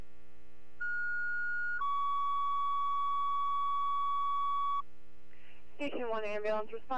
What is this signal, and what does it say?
Two-tone page, heard as a one-second tone, then a three-second tone, about 1440 then 1130 Hz, with speech. It is heard from 0.8 to 4.8 s.